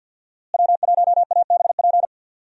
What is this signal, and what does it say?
Morse code, heard as a keyed tone, about 47 wpm, 700 Hz. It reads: W1ABP
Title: USA Call Signs - 5 Characters - Repeated 2x - 50wpm